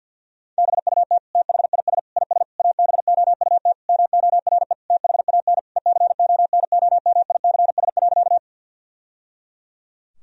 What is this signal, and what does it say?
Morse code, heard as a keyed tone, about 52 wpm, 700 Hz. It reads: BUT THIS IS ABOUT MORE THAN ECONOMICS.
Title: Sentences from Top 1000 Words - Review Part 3 - 50wpm